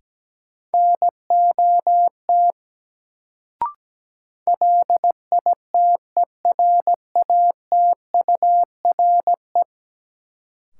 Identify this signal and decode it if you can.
Morse code, heard as a keyed tone, about 17 wpm, 700 Hz. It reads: NOT LITERATURE
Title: Sentences from Top 600 Words - New Word Drill - 17wpm